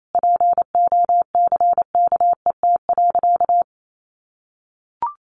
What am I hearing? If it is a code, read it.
Morse code, heard as a keyed tone, about 28 wpm, 700 Hz. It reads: POCKET.
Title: A Sound Of Thunder 28wpm